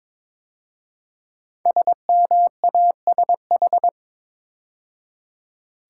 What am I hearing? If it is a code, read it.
Morse code, heard as a keyed tone, about 22 wpm, 700 Hz. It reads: SMASH
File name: 5 Letter Words - First 3 Letters Given - 22wpm